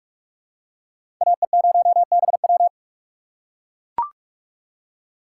Morse code, heard as a keyed tone, about 43 wpm, 700 Hz. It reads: AE0BW
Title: USA Call Signs - 3x Character Spacing - 45wpm